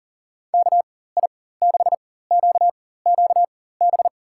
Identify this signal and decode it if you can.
Morse code, heard as a keyed tone, about 40 wpm, spaced as for 19 wpm, 700 Hz. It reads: KI6QQB
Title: USA Call Signs - 6 Characters - 4x Character Spacing - 40wpm